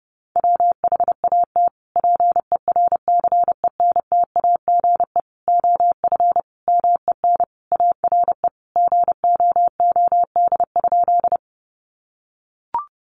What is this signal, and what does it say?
Morse code, heard as a keyed tone, about 30 wpm, 700 Hz. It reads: WHAT PERCENTAGE OF MEN ARE GOOD?
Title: Sentences from Top 1300 Words 30wpm